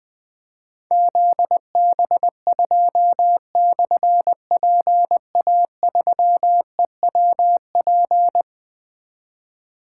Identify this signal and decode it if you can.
Morse code, heard as a keyed tone, about 20 wpm, 700 Hz. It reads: ZB2/PA3EWP
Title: Difficult Call Signs 20wpm